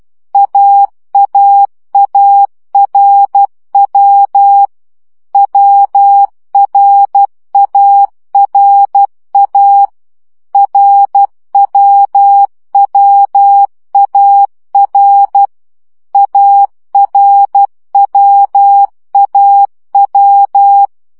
Morse code, heard as a keyed tone, about 12 wpm, 800 Hz. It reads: AAARW WRARA RWWAR ARWAW
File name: CW gyakorlas a r w 10 baud